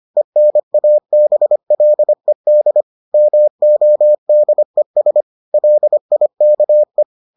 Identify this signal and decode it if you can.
Morse code, heard as a keyed tone, about 25 wpm, 600 Hz. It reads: ENABLED MODES LIKE